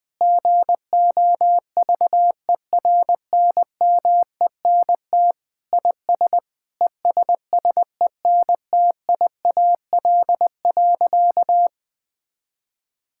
Morse code, heard as a keyed tone, about 21 wpm, 700 Hz. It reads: GOVERNMENT IS ESSENTIAL.
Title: Sentences with 7-Word Phrases - Spoken Beginning - 20wpm